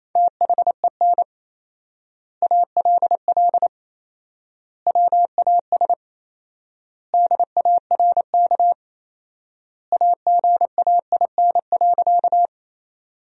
Morse code, heard as a keyed tone, about 28 wpm, 700 Hz. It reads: THEN ALL WAS DARK AGAIN.
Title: 4x Word-Spacing - Sentences from Top 500 Words - 28wpm